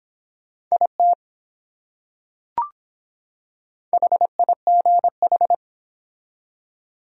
Morse code, heard as a keyed tone, about 24 wpm, 700 Hz. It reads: IT HIGH